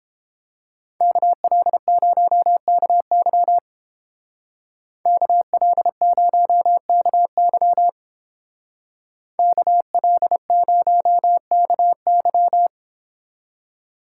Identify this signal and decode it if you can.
Morse code, heard as a keyed tone, about 30 wpm, 700 Hz. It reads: KL0KY KL0KY KL0KY